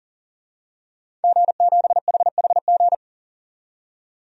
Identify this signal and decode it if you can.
Morse code, heard as a keyed tone, about 40 wpm, 700 Hz. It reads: G7HHG
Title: DXCC-100 Call Signs - 5 Characters - 40wpm